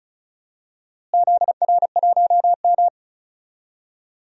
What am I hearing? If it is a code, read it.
Morse code, heard as a keyed tone, about 36 wpm, 700 Hz. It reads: ZR1M
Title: DXCC-100 Call Signs - 4 Characters - Repeated 3x - 35wpm